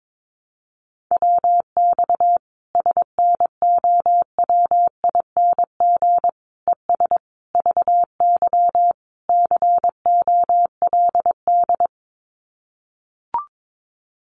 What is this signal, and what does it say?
Morse code, heard as a keyed tone, about 22 wpm, 700 Hz. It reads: WX SNOWING ES VY COLD
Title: Common QSO Phrases 22wpm